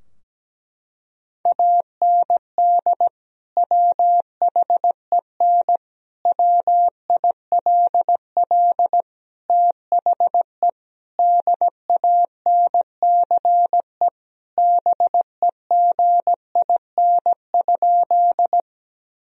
Morse code, heard as a keyed tone, about 17 wpm, 700 Hz. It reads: AND WHEN WILL THE DANCE BEGIN?
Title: Sentences from Top 1000 Words - Review Part 4 - 17wpm